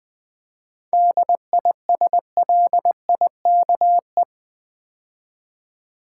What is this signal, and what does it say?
Morse code, heard as a keyed tone, about 20 wpm, 700 Hz. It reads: DISLIKE
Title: 7 Letter Words - First 3 Letters Given - 20wpm